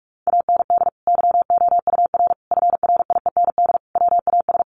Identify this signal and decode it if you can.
Morse code, heard as a keyed tone, about 44 wpm, 700 Hz. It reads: AND YOUR FRIEND WAS
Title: Sentences from Top 1400 Words 45wpm